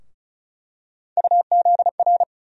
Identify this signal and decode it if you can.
Morse code, heard as a keyed tone, about 35 wpm, 700 Hz. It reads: UZR
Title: USA Call Sign Postfixes - 3 Letters 35wpm